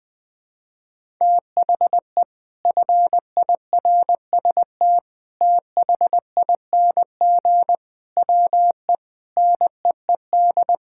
Morse code, heard as a keyed tone, about 20 wpm, 700 Hz. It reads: THE FIRST THING WE NEED
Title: Sentences with 8-Word Phrases - 20wpm